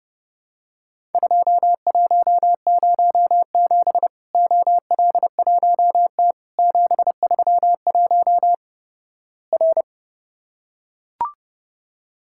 Morse code, heard as a keyed tone, about 29 wpm, 700 Hz. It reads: 2107 OL1T 731 R
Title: Worked All Europe DX Contest - QTC Exchange - 30wpm